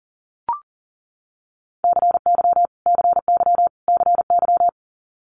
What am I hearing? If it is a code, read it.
Morse code, heard as a keyed tone, about 40 wpm, 690 Hz. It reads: CY CY CY
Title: Multi-letter Phonemes Repeated 3x 40wpm